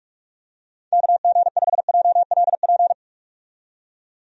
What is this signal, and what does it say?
Morse code, heard as a keyed tone, about 42 wpm, 700 Hz. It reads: KG5JLP
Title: USA Call Signs - 6 Characters - 45wpm